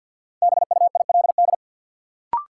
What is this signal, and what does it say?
Morse code, heard as a keyed tone, about 52 wpm, 700 Hz. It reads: BUILD
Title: Sentences from Top 1000 Words - Review Part 1 - New Word Drill - 50wpm